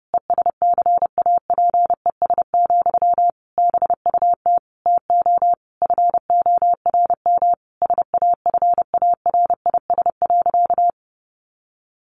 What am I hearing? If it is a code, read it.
Morse code, heard as a keyed tone, about 28 wpm, 700 Hz. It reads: ESCAPES, BUT TO FORM SAFARIS.
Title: A Sound Of Thunder 30wpm